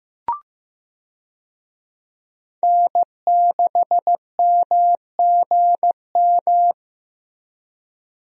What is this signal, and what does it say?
Morse code, heard as a keyed tone, about 15 wpm, 700 Hz. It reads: N6MGM
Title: USA Call Signs - 5 Characters - Repeated 3x - 15wpm